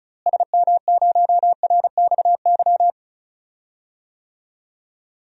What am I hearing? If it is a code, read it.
Morse code, heard as a keyed tone, about 36 wpm, 700 Hz. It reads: SM0RXY
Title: DXCC-100 Call Signs - 6 Characters - Repeated 3x - 35wpm